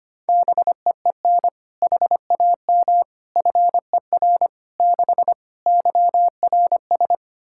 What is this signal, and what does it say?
Morse code, heard as a keyed tone, about 25 wpm, 700 Hz. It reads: BEEN HAM FER 6 YRS
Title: K1USN SST QSO Practice 25wpm